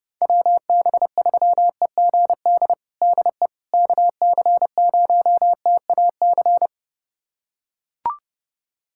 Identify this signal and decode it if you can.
Morse code, heard as a keyed tone, about 30 wpm, 700 Hz. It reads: WB3EGD DE KC0TAC
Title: USA Callsign DE Callsign - 6 Characters - 30wpm